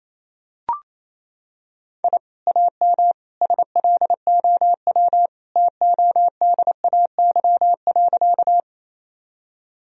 Morse code, heard as a keyed tone, about 28 wpm, 700 Hz. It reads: I AM SLOW TODAY.
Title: Sentences from Top 1000 Words - Review Part 2 - 28wpm